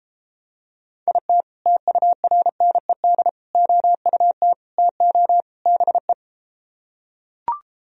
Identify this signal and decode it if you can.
Morse code, heard as a keyed tone, about 34 wpm, 700 Hz. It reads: IT TURNED OUT TO BE